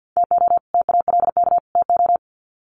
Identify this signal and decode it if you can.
Morse code, heard as a keyed tone, about 50 wpm, 700 Hz. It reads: TO TALK TO
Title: Sets of 3 Words 50wpm